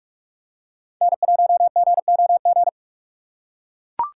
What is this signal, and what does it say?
Morse code, heard as a keyed tone, about 43 wpm, 700 Hz. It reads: N1GOG
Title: USA Call Signs - 2x Character Spacing - 45wpm